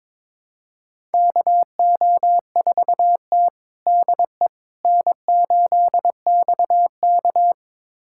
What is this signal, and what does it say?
Morse code, heard as a keyed tone, about 22 wpm, 700 Hz. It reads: KO4T DE N8XK
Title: USA Callsign DE Callsign - 4 Characters - 22wpm